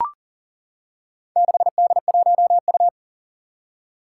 Morse code, heard as a keyed tone, about 40 wpm, 700 Hz. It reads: 6D1U
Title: Four Challenging Letter-Number Combinations 40wpm